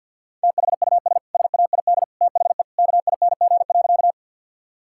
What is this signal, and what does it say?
Morse code, heard as a keyed tone, about 50 wpm, 700 Hz. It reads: THUS SAID THE KING.
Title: Sentences from Top 1000 Words - Review Part 5 - 50wpm